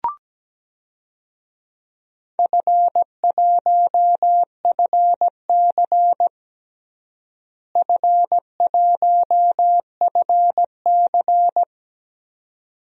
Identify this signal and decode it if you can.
Morse code, heard as a keyed tone, about 17 wpm, 700 Hz. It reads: F1FC F1FC
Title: DXCC-100 Call Signs - 4 Characters - Repeated 2x - 17wpm